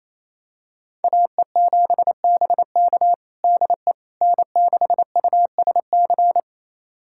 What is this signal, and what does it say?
Morse code, heard as a keyed tone, about 28 wpm, 700 Hz. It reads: AE7BK DE N6USC
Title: USA Callsign DE Callsign - 5 Characters - 28wpm